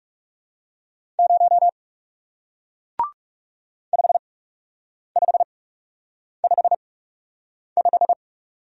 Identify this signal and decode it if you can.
Morse code, heard as a keyed tone, about 41 wpm, spaced as for 8 wpm, 700 Hz. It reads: 0 5555